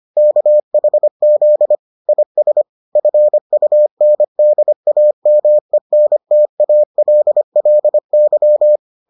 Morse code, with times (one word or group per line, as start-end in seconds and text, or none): 0.2-1.7 KHZ
2.1-2.6 IS
3.0-8.8 FUNDAMENTALLY